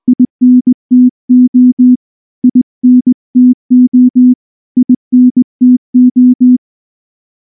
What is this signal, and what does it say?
Morse code, heard as a keyed tone, about 19 wpm, 250 Hz. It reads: INTO INTO INTO
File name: morse / morse-training-20WPM-3